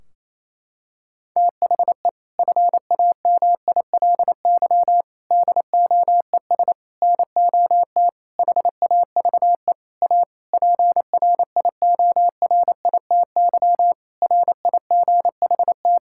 Morse code, heard as a keyed tone, about 28 wpm, 700 Hz. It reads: THE FAMILY DOES NOT HAVE A PRIORITY RIGHT
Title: Sentences from Top 1600 Words 28wpm